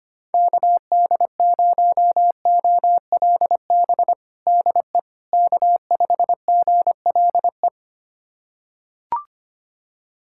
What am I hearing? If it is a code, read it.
Morse code, heard as a keyed tone, about 25 wpm, 700 Hz. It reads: KD0OLB DE K5GLE
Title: USA Callsign DE Callsign 25wpm